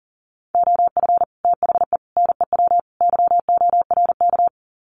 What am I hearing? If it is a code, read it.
Morse code, heard as a keyed tone, about 40 wpm, 700 Hz. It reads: OF THE NEW YORK
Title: Sets of 4 Words 40wpm